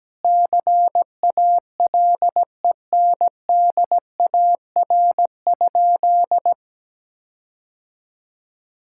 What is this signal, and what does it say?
Morse code, heard as a keyed tone, about 17 wpm, 700 Hz. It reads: CALENDAR?
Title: Sentences from Top 1600 Words 17wpm